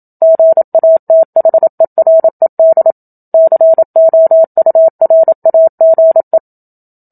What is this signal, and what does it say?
Morse code, heard as a keyed tone, about 27 wpm, 650 Hz. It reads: GATHERED COURAGE
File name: Book Two / 01-Chapter 1 - 27 WPM